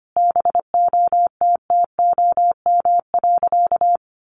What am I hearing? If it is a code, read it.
Morse code, heard as a keyed tone, about 25 wpm, 700 Hz. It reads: BOTTOM.
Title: Sentences from Top 1400 Words 25wpm